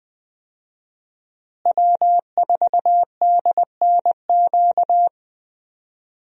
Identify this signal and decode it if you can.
Morse code, heard as a keyed tone, about 19 wpm, 700 Hz. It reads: W4DNQ
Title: USA Call Signs - 5 Characters - Repeated 3x - 20wpm